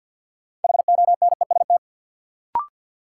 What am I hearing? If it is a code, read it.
Morse code, heard as a keyed tone, about 52 wpm, 700 Hz. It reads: HONEST
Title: Sets of 5 Words Encore - New Word Drill - 50wpm